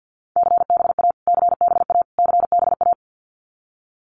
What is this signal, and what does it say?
Morse code, heard as a keyed tone, about 50 wpm, 700 Hz. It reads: CBA CBA CBA
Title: Abbreviations Repeated 3x - 50wpm